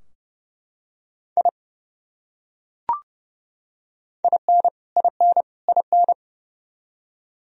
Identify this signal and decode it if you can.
Morse code, heard as a keyed tone, about 31 wpm, 700 Hz. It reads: I IN IN IN
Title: Top 100 Words Repeated 30wpm